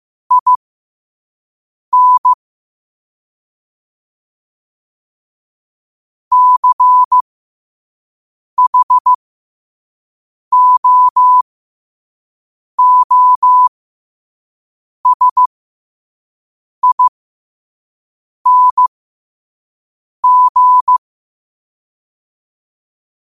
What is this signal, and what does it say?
Morse code, heard as a keyed tone, about 15 wpm, spaced as for 5 wpm, 1000 Hz. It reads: IN CHOOSING
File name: 5wpm